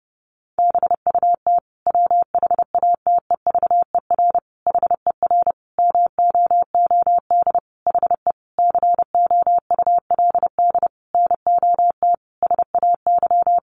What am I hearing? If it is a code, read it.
Morse code, heard as a keyed tone, about 28 wpm, 700 Hz. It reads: BUT WHATEVER HER MOOD HE COULD NOT SAY
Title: Sentences from Top 1100 Words 30wpm